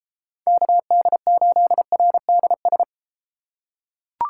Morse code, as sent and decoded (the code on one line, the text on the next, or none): -.- -.. ---.. .-. -.. ...
KD8RDS